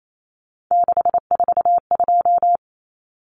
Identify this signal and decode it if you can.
Morse code, heard as a keyed tone, about 28 wpm, 700 Hz. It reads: 642
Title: Numbers - 3-Digit 28wpm